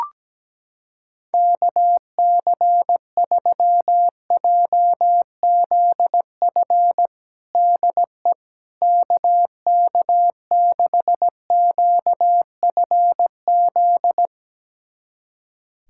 Morse code, heard as a keyed tone, about 17 wpm, 700 Hz. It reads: KC3JZF DE KK6QFZ
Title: USA Callsign DE Callsign 17wpm